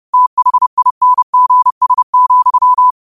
Morse code, heard as a keyed tone, about 30 wpm, 1000 Hz. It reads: THINGS,
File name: qotdfiles / 30wpm